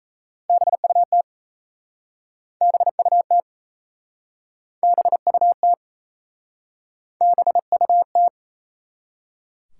Morse code, heard as a keyed tone, about 37 wpm, 700 Hz. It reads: BUT BUT BUT BUT